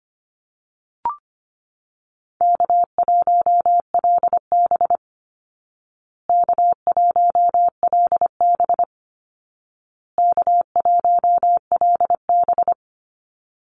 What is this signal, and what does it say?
Morse code, heard as a keyed tone, about 25 wpm, 700 Hz. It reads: K1LB K1LB K1LB
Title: Easy Call Signs Repeated 3x - 25wpm